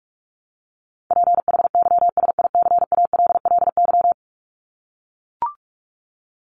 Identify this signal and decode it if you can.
Morse code, heard as a keyed tone, about 44 wpm, 700 Hz. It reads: PHYSICALLY
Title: Top 1501-1750 Words 45wpm